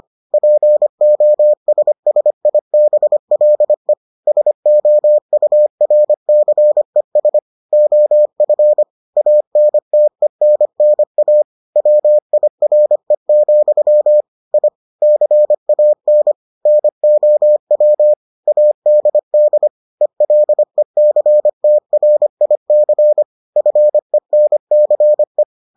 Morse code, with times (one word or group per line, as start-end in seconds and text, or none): 0.3-3.9 POSSIBLE
4.3-7.4 SOURCES
7.7-8.8 OF
9.2-11.4 ANTENNA
11.8-14.2 WIRE,
14.5-14.7 I
15.0-16.3 CAN
16.7-18.1 NOW
18.5-19.7 ADD
20.0-23.2 ELECTRIC
23.6-25.4 FENCE